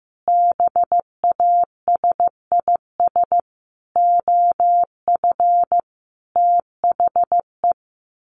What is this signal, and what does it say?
Morse code, heard as a keyed tone, about 15 wpm, 700 Hz. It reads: BASIS OF THE
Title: Sets of 5 Words 15wpm